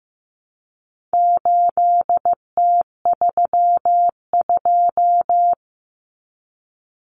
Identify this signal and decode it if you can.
Morse code, heard as a keyed tone, about 15 wpm, 700 Hz. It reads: 8T32
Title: Common Cut Numbers - 4-Digit 15wpm